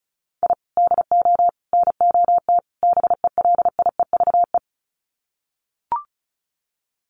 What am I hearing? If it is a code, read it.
Morse code, heard as a keyed tone, about 34 wpm, 700 Hz. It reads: I DO NOT BELIEVE
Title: Sets of 4 Words 35wpm